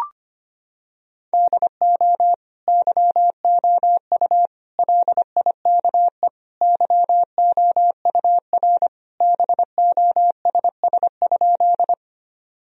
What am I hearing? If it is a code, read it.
Morse code, heard as a keyed tone, about 25 wpm, 700 Hz. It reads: DO YOU LIKE YOUR BOSS?
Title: Sentences from Top 1000 Words - Review Part 4 - 25wpm